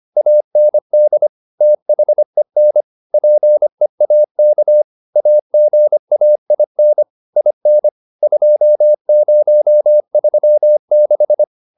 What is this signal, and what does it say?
Morse code, heard as a keyed tone, about 25 wpm, 600 Hz. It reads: AND THEN PEAK AGAIN IN 2036
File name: SS311 25